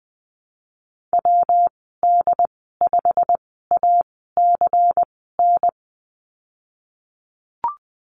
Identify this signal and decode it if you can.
Morse code, heard as a keyed tone, about 20 wpm, spaced as for 15 wpm, 700 Hz. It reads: WD5ACN
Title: 167 Lesson 27g - Number 7 - Call Signs 20wpm